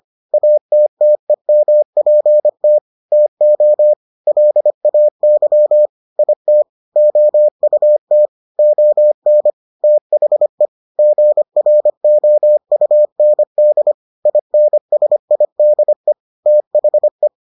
Morse code, heard as a keyed tone, about 25 wpm, 600 Hz. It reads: ATTEMPT TO LAY IT OUT ON THE GROUND INSIDE THE